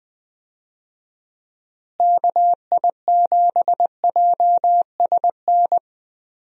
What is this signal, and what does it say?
Morse code, heard as a keyed tone, about 20 wpm, 700 Hz. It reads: KI7JSN
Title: USA Call Signs - 6 Characters - Repeated 3x - 20wpm